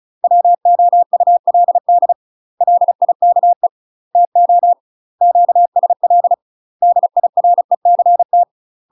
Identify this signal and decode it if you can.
Morse code, heard as a keyed tone, about 35 wpm, 700 Hz. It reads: WOULD LIKE TO QSL DIRECT
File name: UK MASTER QSOs TRAINING / 35WPM / UK MASTER QSOs 0012